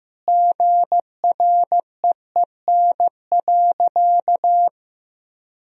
Morse code, heard as a keyed tone, about 15 wpm, 700 Hz. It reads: GREEN.
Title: Sentences from Top 1000 Words - Review Part 2 - 15wpm